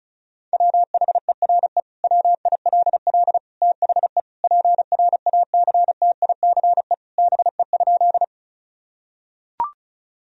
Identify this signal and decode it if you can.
Morse code, heard as a keyed tone, about 36 wpm, 700 Hz. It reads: WHERE WILL THE PRACTICE BE?
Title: Sentences from Top 1000 Words - Review Part 4 - 35wpm